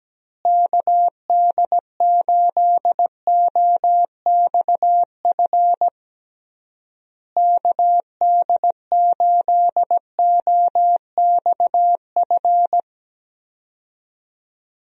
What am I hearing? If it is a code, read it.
Morse code, heard as a keyed tone, about 17 wpm, 700 Hz. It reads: KD8OXF KD8OXF
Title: USA Call Signs - 6 Characters - Repeated 3x - 17wpm